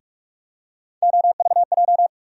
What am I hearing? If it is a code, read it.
Morse code, heard as a keyed tone, about 44 wpm, 700 Hz. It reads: OVJ
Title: USA Call Sign Postfixes - 3 Letters 45wpm